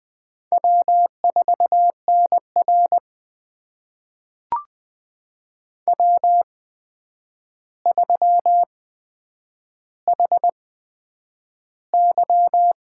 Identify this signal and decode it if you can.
Morse code, heard as a keyed tone, about 20 wpm, spaced as for 9 wpm, 700 Hz. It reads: W4NR W3HY
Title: USA Call Signs - 4 Characters - 8x Character Spacing - 20wpm